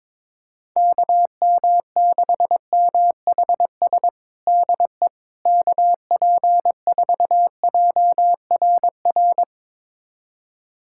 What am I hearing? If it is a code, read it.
Morse code, heard as a keyed tone, about 22 wpm, 700 Hz. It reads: KM6MHS DE KP4JRR
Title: USA Callsign DE Callsign - 6 Characters - 22wpm